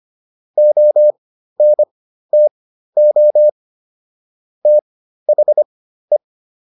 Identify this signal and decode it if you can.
Morse code, heard as a keyed tone, about 25 wpm, spaced as for 13 wpm, 600 Hz. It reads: ONTO THE